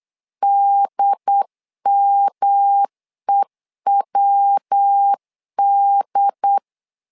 Morse code, heard as a keyed tone, about 8 wpm, 800 Hz. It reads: DMEWD
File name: LEZIONI / 3 MWRO / 3 MW 8wpm 2501